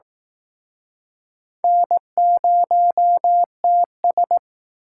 Morse code, heard as a keyed tone, about 19 wpm, 700 Hz. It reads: N0TS